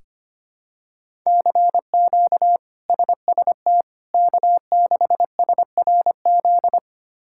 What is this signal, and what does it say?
Morse code, heard as a keyed tone, about 25 wpm, 700 Hz. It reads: CQ SST K6SRZ
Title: K1USN SST QSO Practice - 25wpm